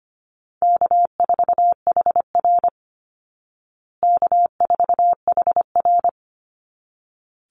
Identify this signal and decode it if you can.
Morse code, heard as a keyed tone, about 25 wpm, 700 Hz. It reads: K4HR K4HR
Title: Easy Call Signs Repeated 2x - 25wpm